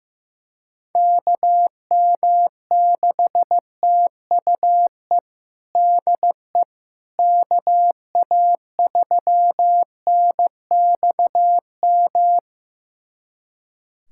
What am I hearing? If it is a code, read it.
Morse code, heard as a keyed tone, about 16 wpm, 700 Hz. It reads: KM6TUE DE KA3NXM
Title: USA Callsign DE Callsign 15wpm